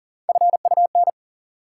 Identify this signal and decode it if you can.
Morse code, heard as a keyed tone, about 40 wpm, 700 Hz. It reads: FUN
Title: Sentences from Top 700 Words - New Word Drill - 40wpm